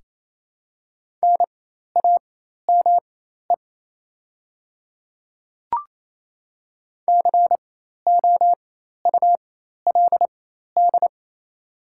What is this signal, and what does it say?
Morse code, heard as a keyed tone, about 28 wpm, spaced as for 13 wpm, 690 Hz. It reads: NAME COULD
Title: 4x Character Spacing - Top 200 Words - 28wpm